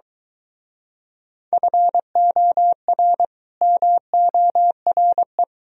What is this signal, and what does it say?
Morse code, heard as a keyed tone, about 23 wpm, 700 Hz. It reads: FOR MORE